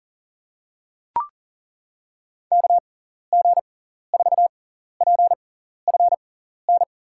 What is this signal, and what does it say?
Morse code, heard as a keyed tone, about 40 wpm, spaced as for 14 wpm, 700 Hz. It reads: KG4PFN
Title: USA Call Signs - 6 Characters - 6x Character Spacing - 40wpm